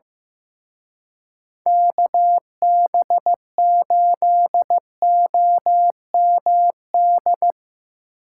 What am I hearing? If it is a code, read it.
Morse code, heard as a keyed tone, about 17 wpm, 700 Hz. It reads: KB8OMD
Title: Speed Racing - USA 2x3 Call Signs 15wpm